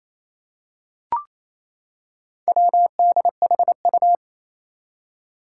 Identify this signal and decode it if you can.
Morse code, heard as a keyed tone, about 28 wpm, 700 Hz. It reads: WDHU
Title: Four Challenging Letter-Number Combinations 28wpm